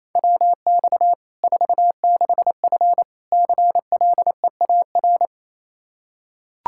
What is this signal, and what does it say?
Morse code, heard as a keyed tone, about 28 wpm, 700 Hz. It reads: WX 46F CLEAR
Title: Practice QSO - Exchange 2 - Info - 28wpm